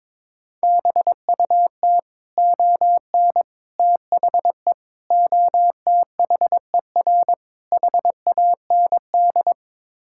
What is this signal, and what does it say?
Morse code, heard as a keyed tone, about 22 wpm, 700 Hz. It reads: BUT ON THE OTHER HAND
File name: practice / 22wpm / Build A Phrase - 5-Word Phrases - 22wpm